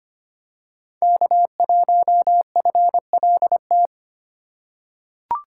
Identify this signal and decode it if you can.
Morse code, heard as a keyed tone, about 25 wpm, 700 Hz. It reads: K1FLT
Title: USA Call Signs - 5 Characters - 2x Character Spacing - 25wpm